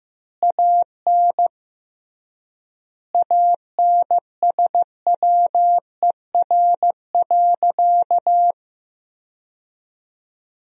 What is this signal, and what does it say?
Morse code, heard as a keyed tone, about 15 wpm, 700 Hz. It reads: AN ANSWER.
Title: 3x Word-Spacing - Sentences from Top 300 Words - 15wpm